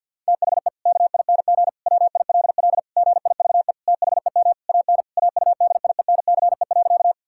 Morse code, heard as a keyed tone, about 50 wpm, 700 Hz. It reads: THE KING WILL GIVE THEM AN AUDIENCE.
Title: Sentences from Top 1000 Words - Review Part 5 - 50wpm